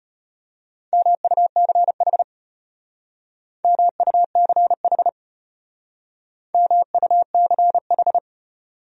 Morse code, heard as a keyed tone, about 31 wpm, 700 Hz. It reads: MUCH MUCH MUCH
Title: Speed-Racing Top 500 Words 25wpm